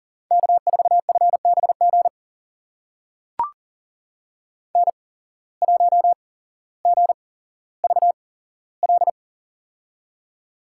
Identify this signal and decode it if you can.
Morse code, heard as a keyed tone, about 38 wpm, spaced as for 17 wpm, 700 Hz. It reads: K4FBG N1GVL
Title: USA Call Signs - 5 Characters - 8x Character Spacing - 40wpm